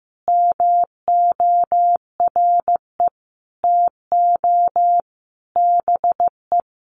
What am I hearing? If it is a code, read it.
Morse code, heard as a keyed tone, about 15 wpm, 700 Hz. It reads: MORE TO BE
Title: Sentences from Top 1000 Words 15wpm